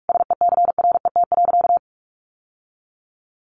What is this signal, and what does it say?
Morse code, heard as a keyed tone, about 44 wpm, 700 Hz. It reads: SECRET.
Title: Sentences from Top 1200 Words 45wpm